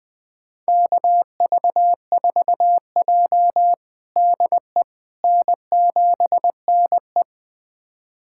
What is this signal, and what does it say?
Morse code, heard as a keyed tone, about 20 wpm, 700 Hz. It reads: KV4J DE N7NE
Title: USA Callsign DE Callsign - 4 Characters - 20wpm